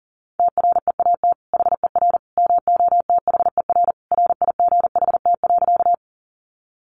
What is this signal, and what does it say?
Morse code, heard as a keyed tone, about 40 wpm, 700 Hz. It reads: TREAT HER MOTHER RIGHT.
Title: Sentences from Top 1300 Words 40wpm